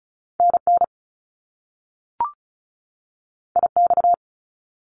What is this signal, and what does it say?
Morse code, heard as a keyed tone, about 36 wpm, 700 Hz. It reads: NN IX
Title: Top Two Letter Combinations in Words 35wpm